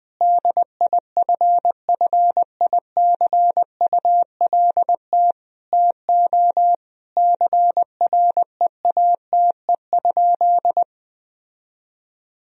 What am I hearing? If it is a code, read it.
Morse code, heard as a keyed tone, about 20 wpm, 700 Hz. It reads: DIFFICULT TO CREATE?
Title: Sentences from Top 1000 Words - Review Part 3 - 20wpm